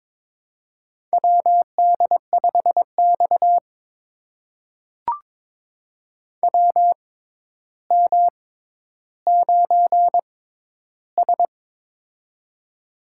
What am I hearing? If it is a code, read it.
Morse code, heard as a keyed tone, about 22 wpm, spaced as for 11 wpm, 700 Hz. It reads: WD5X WM9S
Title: USA Call Signs - 4 Characters - 6x Character Spacing - 22wpm